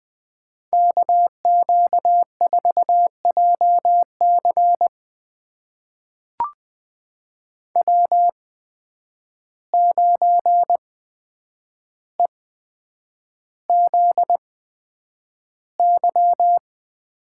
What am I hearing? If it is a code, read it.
Morse code, heard as a keyed tone, about 20 wpm, spaced as for 9 wpm, 700 Hz. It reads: KQ4JC W9EZY
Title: USA Call Signs - 5 Characters - 8x Character Spacing - 20wpm